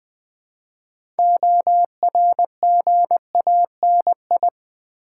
Morse code, heard as a keyed tone, about 20 wpm, 700 Hz. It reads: ORGANI
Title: Build A Word - 3 Letters At A Time - 8-Letter Words - 20wpm